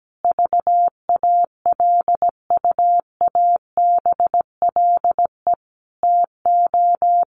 Morse code, heard as a keyed tone, about 17 wpm, 700 Hz. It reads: VALUABLE TO
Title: Sentences from Top 1100 Words 17wpm